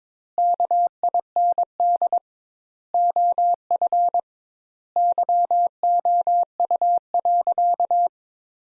Morse code, heard as a keyed tone, about 22 wpm, 700 Hz. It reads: KIND OF YOU.
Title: Sentences Top 200 Words - 22wpm